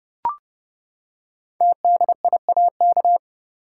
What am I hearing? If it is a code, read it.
Morse code, heard as a keyed tone, about 28 wpm, 700 Hz. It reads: TDIAK
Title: Code Groups - 5 Letters - 30wpm